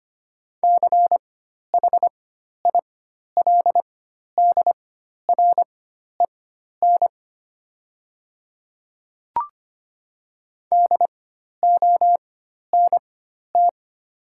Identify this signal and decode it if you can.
Morse code, heard as a keyed tone, about 25 wpm, spaced as for 12 wpm, 690 Hz. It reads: CHILDREN DONT
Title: 4x Character Spacing - Top 300 Words - 25wpm